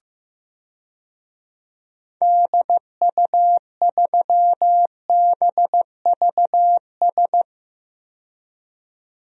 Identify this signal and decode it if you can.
Morse code, heard as a keyed tone, about 15 wpm, 700 Hz. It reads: DU3BVS
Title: DXCC-100 Call Signs - 6 Characters - 15wpm